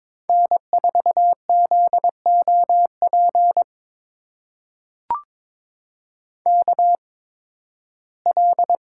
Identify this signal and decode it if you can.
Morse code, heard as a keyed tone, about 22 wpm, spaced as for 14 wpm, 700 Hz. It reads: N4ZOP KL
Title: USA Call Signs - 8x Character Spacing - 22wpm